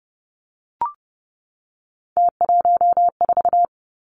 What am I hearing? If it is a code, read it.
Morse code, heard as a keyed tone, about 32 wpm, 700 Hz. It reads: T14
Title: Three Challenging Letter-Number Combinations 30wpm